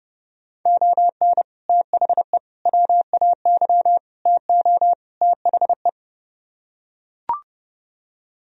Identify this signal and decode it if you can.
Morse code, heard as a keyed tone, about 30 wpm, 700 Hz. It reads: ON THE WAY TO THE